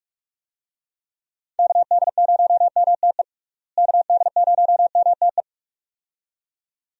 Morse code, heard as a keyed tone, about 41 wpm, 700 Hz. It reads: KD0MTE KD0MTE
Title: USA Call Signs - 6 Characters - Repeated 2x - 45wpm